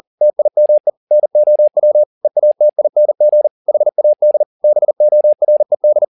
Morse code, heard as a keyed tone, about 40 wpm, 600 Hz. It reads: TIME NOW EATING HAD BORED